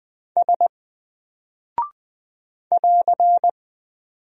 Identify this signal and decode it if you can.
Morse code, heard as a keyed tone, about 20 wpm, 700 Hz. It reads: S <AR>
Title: Single Character (Letter:Number:Punctuation:Prosign) 20wpm